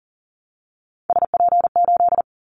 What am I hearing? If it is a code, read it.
Morse code, heard as a keyed tone, about 40 wpm, 700 Hz. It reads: SP8
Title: Callsign Prefixes for Top 25 DXCC - Repeated 3x - 40wpm